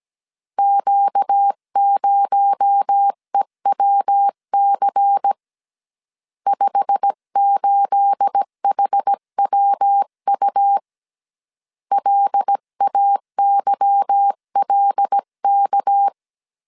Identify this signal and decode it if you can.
Morse code, heard as a keyed tone, about 17 wpm, 800 Hz. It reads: Q0EWC 58HWU LAYLK
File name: CW / DET / 1 9 1500C S B 16wpm 2856